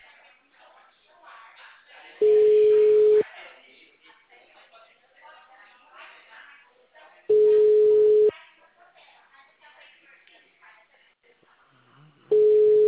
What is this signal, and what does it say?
Ringback tone (Europe, 425 Hz), with speech.
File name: out- -2016- - - .513